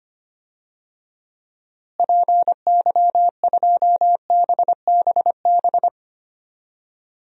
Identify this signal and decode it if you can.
Morse code, heard as a keyed tone, about 25 wpm, 700 Hz. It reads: PY2BBB
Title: DXCC-100 Call Signs - 6 Characters - Repeated 2x - 25wpm